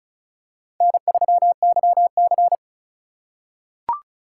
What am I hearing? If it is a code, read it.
Morse code, heard as a keyed tone, about 36 wpm, 690 Hz. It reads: N3YC
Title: USA Call Signs - 4 Characters - 2x Character Spacing - 35wpm